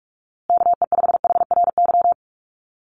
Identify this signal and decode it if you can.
Morse code, heard as a keyed tone, about 44 wpm, 700 Hz. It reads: KE5HRY
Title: USA Call Signs Repeated 2x 45wpm